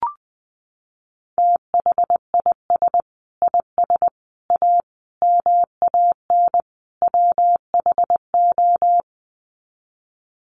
Morse code, heard as a keyed tone, about 20 wpm, 700 Hz. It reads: THIS IS A MAN WHO
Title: Sets of 5 Words Encore2 20wpm